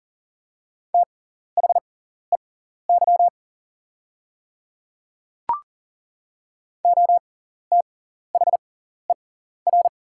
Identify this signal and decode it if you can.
Morse code, heard as a keyed tone, about 39 wpm, spaced as for 14 wpm, 700 Hz. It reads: THEY OTHER
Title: 6x Character Spacing - Top 200 Words - 40wpm